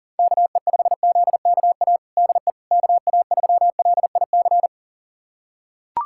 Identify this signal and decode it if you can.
Morse code, heard as a keyed tone, about 40 wpm, 700 Hz. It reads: KE5ZKA DE KA3LIC
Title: USA Callsign DE Callsign 40wpm